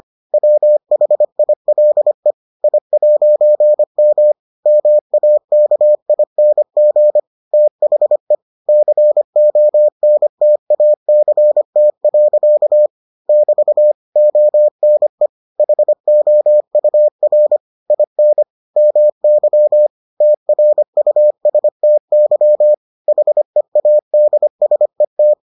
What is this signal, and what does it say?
Morse code, heard as a keyed tone, about 25 wpm, 600 Hz. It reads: WHILE I'M MAKING THE CONTACT. <BT> ONE HOUR IN MY TRUSTY HEADSET